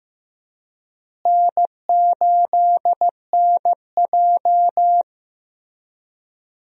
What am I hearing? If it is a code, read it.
Morse code, heard as a keyed tone, about 15 wpm, 700 Hz. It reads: N8NJ
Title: USA Call Signs - 4 Characters - Repeated 2x - 15wpm